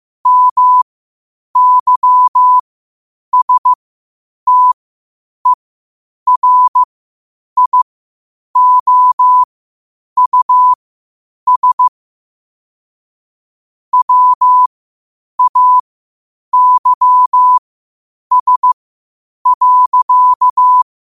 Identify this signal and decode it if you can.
Morse code, heard as a keyed tone, about 15 wpm, spaced as for 8 wpm, 1000 Hz. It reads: MYSTERIOUS WAYS.
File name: qotdfiles / 8wpm